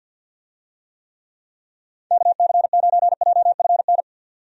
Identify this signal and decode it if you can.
Morse code, heard as a keyed tone, about 46 wpm, 700 Hz. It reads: KC9JFN
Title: USA Call Signs - 6 Characters - 50wpm